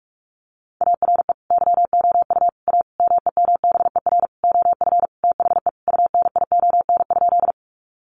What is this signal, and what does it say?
Morse code, heard as a keyed tone, about 44 wpm, 700 Hz. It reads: ARE YOU A MEMBER OF THE UNION?
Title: Sentences from Top 1400 Words 45wpm